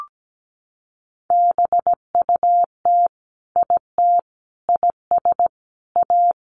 Morse code, heard as a keyed tone, about 17 wpm, 700 Hz. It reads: BUT IT IS A
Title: Sentences from Top 1500 Words 17wpm